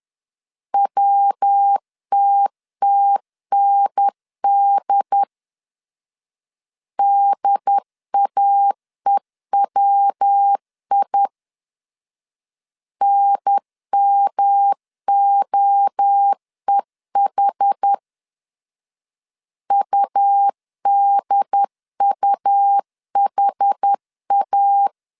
Morse code, heard as a keyed tone, about 10 wpm, 800 Hz. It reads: WTTND DAEWI NMOEH UDUHA
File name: CW / DET / 1 3 1500C S C 10wpm 3335